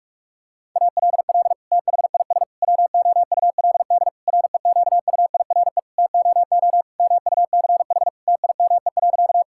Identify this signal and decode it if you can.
Morse code, heard as a keyed tone, about 42 wpm, 700 Hz. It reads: ALL THIS WOULD REQUIRE TOO MUCH TIME.
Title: Sentences from Top 1000 Words - Review Part 6 - 45wpm